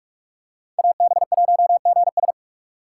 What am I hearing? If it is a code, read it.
Morse code, heard as a keyed tone, about 46 wpm, 700 Hz. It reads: AB1GS
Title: USA Call Signs - 5 Characters - 4x Character Spacing - 45wpm